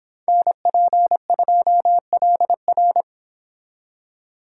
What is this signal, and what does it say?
Morse code, heard as a keyed tone, about 25 wpm, 700 Hz. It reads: NP2LR